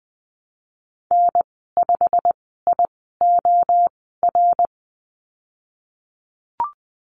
Morse code, heard as a keyed tone, about 20 wpm, spaced as for 15 wpm, 700 Hz. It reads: N5IOR
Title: 173 Lesson 28g - Number 9 - Call Signs 20wpm